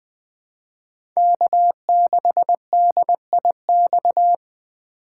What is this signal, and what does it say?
Morse code, heard as a keyed tone, about 20 wpm, 700 Hz. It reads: K6DIX
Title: USA Call Signs - 5 Characters - 2x Character Spacing - 20wpm